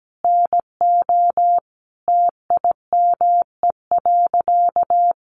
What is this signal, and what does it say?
Morse code, heard as a keyed tone, about 17 wpm, 700 Hz. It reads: NO TIME.
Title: Sentences from Top 100 Words 17wpm